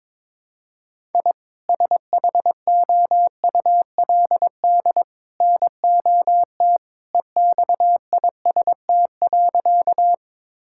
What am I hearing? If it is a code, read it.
Morse code, heard as a keyed tone, about 22 wpm, 700 Hz. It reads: I SHOULD NOT EXIST.
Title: Sentences from Top 1000 Words - Review Part 5 - 22wpm